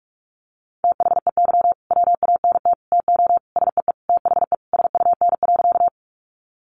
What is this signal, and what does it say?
Morse code, heard as a keyed tone, about 44 wpm, 700 Hz. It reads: THEY WANT TO SEE THE SUN.
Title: Easy Phrases 45wpm